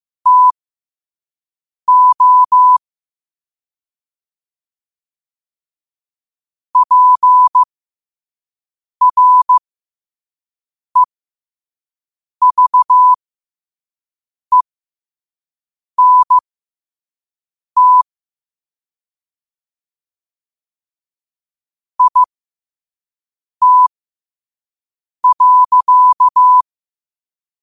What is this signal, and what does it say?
Morse code, heard as a keyed tone, about 15 wpm, spaced as for 5 wpm, 1000 Hz. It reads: TO PREVENT IT.
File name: qotdfiles / 5wpm